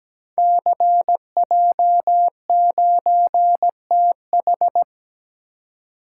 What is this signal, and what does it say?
Morse code, heard as a keyed tone, about 17 wpm, 700 Hz. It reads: CJ9TH
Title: Code Groups - 5 Letter:Numbers - 17wpm